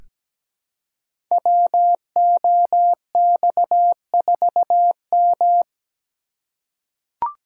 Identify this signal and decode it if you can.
Morse code, heard as a keyed tone, about 17 wpm, 700 Hz. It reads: WOX4M
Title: Code Groups - 5 Letter:Numbers - 17wpm